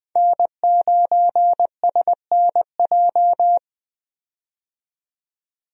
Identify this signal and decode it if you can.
Morse code, heard as a keyed tone, about 20 wpm, 700 Hz. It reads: N9SNJ
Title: USA Call Signs - 5 Characters - 20wpm